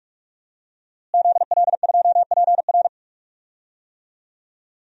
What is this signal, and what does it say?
Morse code, heard as a keyed tone, about 41 wpm, 700 Hz. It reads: ZL2PR
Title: DXCC-100 Call Signs - 5 Characters - Repeated 3x - 45wpm